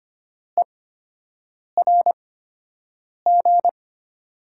Morse code, heard as a keyed tone, about 25 wpm, spaced as for 7 wpm, 700 Hz. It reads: ERG
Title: USA Call Signs - 6 Characters - 8x Character Spacing - 25wpm